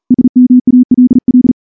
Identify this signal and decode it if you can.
Morse code, heard as a keyed tone, about 42 wpm, 250 Hz. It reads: SMALL
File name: morse-training-40WPM